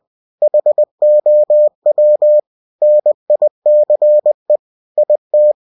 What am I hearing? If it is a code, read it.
Morse code, heard as a keyed tone, about 20 wpm, 600 Hz. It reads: HOW NICE IT
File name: cw-books / winnie-the-pooh / winnie-20wpm-page-0023